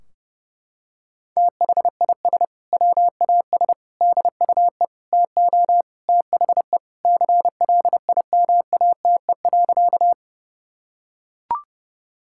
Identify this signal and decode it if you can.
Morse code, heard as a keyed tone, about 30 wpm, 700 Hz. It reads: THIS WAS DUE TO THE CLIMATE.
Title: Sentences from Top 1600 Words 30wpm